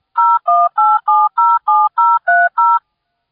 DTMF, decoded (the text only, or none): 018707030